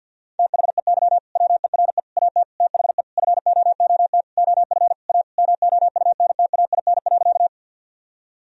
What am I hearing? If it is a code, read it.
Morse code, heard as a keyed tone, about 50 wpm, 700 Hz. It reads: THEY WERE AT THE FOOT OF A MOUNTAIN.
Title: Sentences from Top 1000 Words - Review Part 3 - 50wpm